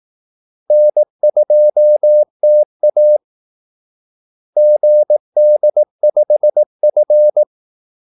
Morse code, heard as a keyed tone, about 18 wpm, 600 Hz. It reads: N2TA GD5F
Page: Kaldesignaler 18 wpm